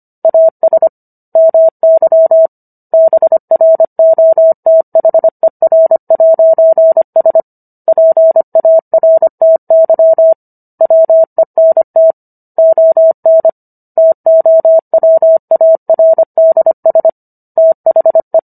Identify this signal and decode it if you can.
Morse code, heard as a keyed tone, about 25 wpm, 650 Hz. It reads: AS MY BROTHER'S PARTY WENT ON TOWARDS THE